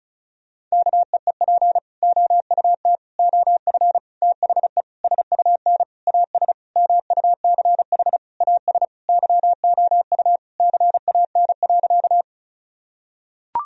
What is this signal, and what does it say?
Morse code, heard as a keyed tone, about 36 wpm, 700 Hz. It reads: KEEP OUT OF THE SUN AS MUCH AS YOU CAN.
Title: Sentences from Top 1000 Words - Review Part 1 - 35wpm